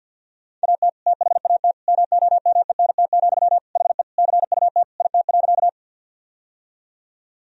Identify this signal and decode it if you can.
Morse code, heard as a keyed tone, about 52 wpm, 700 Hz. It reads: AT THAT MOMENT, HE CUT IT.
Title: Sentences from Top 1000 Words - Review Part 1 - 50wpm